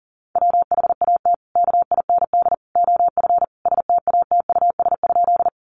Morse code, heard as a keyed tone, about 40 wpm, 700 Hz. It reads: WHAT KIND OF STATUS?
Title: Sentences from Top 1400 Words 40wpm